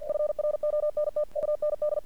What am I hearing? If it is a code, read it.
Morse code, heard as a keyed tone, about 48 wpm, 600 Hz. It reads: FRONT UND
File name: archived / CRSnet